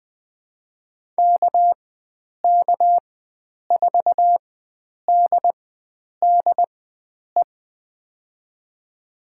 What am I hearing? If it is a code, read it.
Morse code, heard as a keyed tone, about 20 wpm, spaced as for 9 wpm, 700 Hz. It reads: KK4DDE
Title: USA Call Signs - 6 Characters - 4x Character Spacing - 20wpm